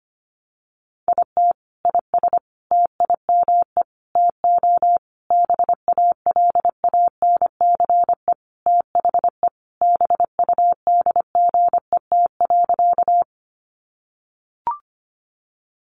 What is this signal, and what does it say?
Morse code, heard as a keyed tone, about 25 wpm, 700 Hz. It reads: IT IS TIME TO BALANCE THE BUDGET.
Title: Sentences from Top 1200 Words 25wpm